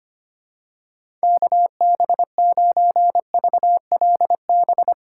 Morse code, heard as a keyed tone, about 25 wpm, 700 Hz. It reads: KB9VLB
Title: USA Call Signs - 6 Characters - 4x Character Spacing - 25wpm